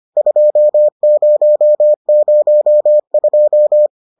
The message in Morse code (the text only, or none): 2002